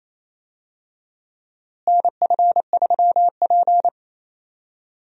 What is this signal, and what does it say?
Morse code, heard as a keyed tone, about 28 wpm, 700 Hz. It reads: NF3P
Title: USA Call Signs - 4 Characters - Repeated 2x - 28wpm